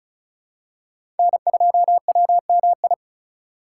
Morse code, heard as a keyed tone, about 36 wpm, 700 Hz. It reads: N2WMI
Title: USA Call Signs - 5 Characters - 4x Character Spacing - 35wpm